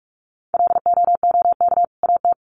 Morse code, heard as a keyed tone, about 47 wpm, 700 Hz. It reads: LOOK AT
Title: Sets of 2 Words 45wpm